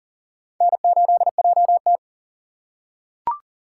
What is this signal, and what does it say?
Morse code, heard as a keyed tone, about 40 wpm, 700 Hz. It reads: N8JT
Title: USA Call Signs - 4 Characters - 2x Character Spacing - 40wpm